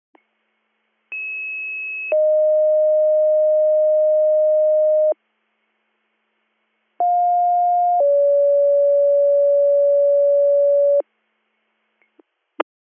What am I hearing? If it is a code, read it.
Two-tone page, heard as a one-second tone, then a three-second tone, about 2600 then 620 Hz, about 710 then 570 Hz. It is heard from 1.1 to 11.0 s.